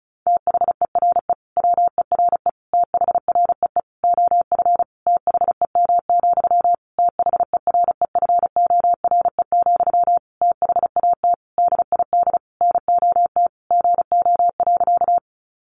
Morse code, heard as a keyed tone, about 34 wpm, 700 Hz. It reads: THERE WERE THREE OF THEM, THEREFORE, THAT DID NOT GO.
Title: Sentences from Top 800 Words 35wpm